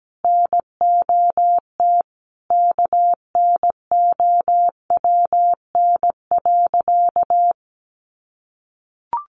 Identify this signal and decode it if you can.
Morse code, heard as a keyed tone, about 17 wpm, 700 Hz. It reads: NOT KNOWN.
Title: Sentences from Top 1100 Words 17wpm